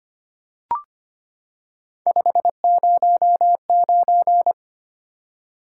Morse code, heard as a keyed tone, about 25 wpm, 700 Hz. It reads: 509
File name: Three Letter-Number Combinations - 25wpm